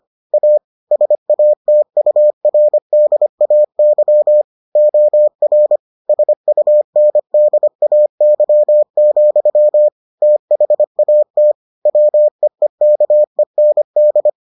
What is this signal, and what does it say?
Morse code, heard as a keyed tone, about 25 wpm, 600 Hz. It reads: A SATURDAY OR SUNDAY, THAT WEEKEND